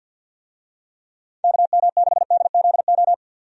Morse code, heard as a keyed tone, about 49 wpm, 700 Hz. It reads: KM6DZO
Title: USA Call Signs - 6 Characters - 50wpm